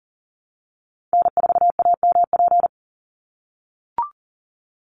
Morse code, heard as a keyed tone, about 40 wpm, 700 Hz. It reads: N4AMP
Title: USA Call Signs 40wpm